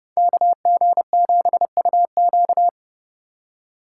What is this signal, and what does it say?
Morse code, heard as a keyed tone, about 30 wpm, 700 Hz. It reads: KG7UQ